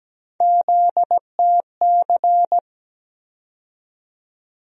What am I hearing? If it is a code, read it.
Morse code, heard as a keyed tone, about 17 wpm, 700 Hz. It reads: ZTC
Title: USA Call Sign Postfixes - 3 Letters 17wpm